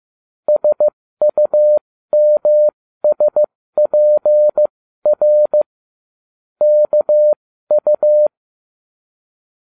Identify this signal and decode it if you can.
Morse code, heard as a keyed tone, about 15 wpm, spaced as for 13 wpm, 600 Hz. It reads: SUMSPR KU